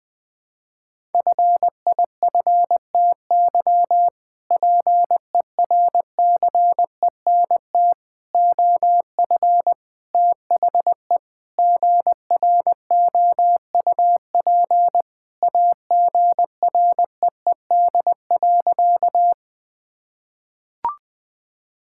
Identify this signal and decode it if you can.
Morse code, heard as a keyed tone, about 20 wpm, 700 Hz. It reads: FIFTY PERCENT OF THE GROUP AGREED.
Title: Sentences with 3-Word Phrases - Spoken Beginning - 20wpm